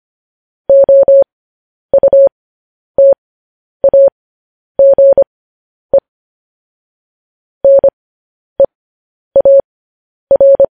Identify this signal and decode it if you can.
Morse code, heard as a keyed tone, about 25 wpm, spaced as for 10 wpm, 550 Hz. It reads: OUTAGE NEAR
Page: News Headlines in Morse Code at 10 WPM